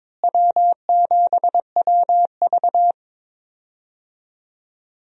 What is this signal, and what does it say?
Morse code, heard as a keyed tone, about 22 wpm, 700 Hz. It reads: W7WV
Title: USA Call Signs - 4 Characters - Repeated 3x - 22wpm